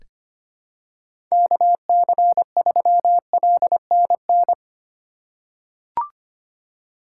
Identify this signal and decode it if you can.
Morse code, heard as a keyed tone, about 25 wpm, 700 Hz. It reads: KC3LNN
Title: USA Call Signs - 2x Character Spacing - 25wpm